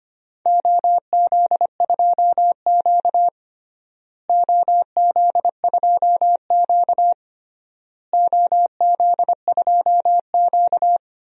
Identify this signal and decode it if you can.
Morse code, heard as a keyed tone, about 25 wpm, 700 Hz. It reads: OZ2Q OZ2Q OZ2Q
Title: DXCC-100 Call Signs - 4 Characters - Repeated 3x - 25wpm